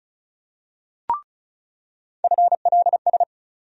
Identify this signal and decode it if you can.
Morse code, heard as a keyed tone, about 35 wpm, 700 Hz. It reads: FLS
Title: USA Call Sign Postfixes - 3 Letters 35wpm